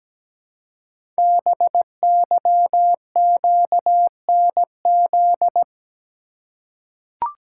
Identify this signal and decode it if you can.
Morse code, heard as a keyed tone, about 17 wpm, 700 Hz. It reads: BYQNZ
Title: Code Groups - 5 Letters - 17wpm